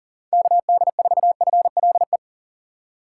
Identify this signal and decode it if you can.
Morse code, heard as a keyed tone, about 37 wpm, 700 Hz. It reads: KD4FLE
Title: USA Call Signs - 6 Characters - 4x Character Spacing - 40wpm